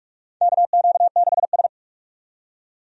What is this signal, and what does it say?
Morse code, heard as a keyed tone, about 46 wpm, 700 Hz. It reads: KQ6S
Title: USA Call Signs - 4 Characters - 45wpm